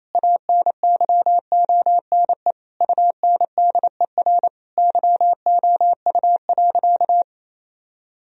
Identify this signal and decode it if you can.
Morse code, heard as a keyed tone, about 28 wpm, 700 Hz. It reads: ANYONE UNDER YOU.
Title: Sentences from Top 1000 Words - Review Part 1 - 28wpm